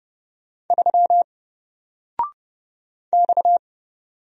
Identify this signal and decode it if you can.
Morse code, heard as a keyed tone, about 30 wpm, 700 Hz. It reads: 3 X
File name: Single-Letter-Number-30wpm